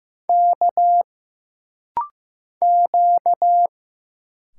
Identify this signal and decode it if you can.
Morse code, heard as a keyed tone, about 15 wpm, 700 Hz. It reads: K Q
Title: Single Letter-Number Alt - Rapid-Fire 15wpm